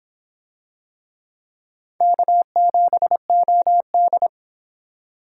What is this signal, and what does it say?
Morse code, heard as a keyed tone, about 23 wpm, 700 Hz. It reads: K7OD